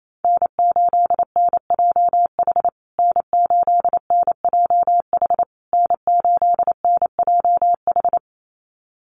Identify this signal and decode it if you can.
Morse code, heard as a keyed tone, about 28 wpm, 700 Hz. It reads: N8NJH N8NJH N8NJH
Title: USA Call Signs Repeated 3x 28wpm